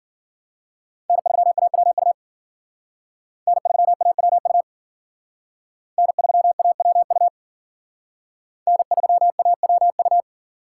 Morse code, heard as a keyed tone, about 47 wpm, 700 Hz. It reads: N3AWU N3AWU N3AWU N3AWU
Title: Speed Racing - USA 1x3 Call Signs 40wpm